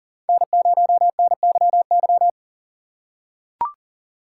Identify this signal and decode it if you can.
Morse code, heard as a keyed tone, about 39 wpm, 700 Hz. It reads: N0NYY
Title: USA Call Signs - 5 Characters - 3x Character Spacing - 40wpm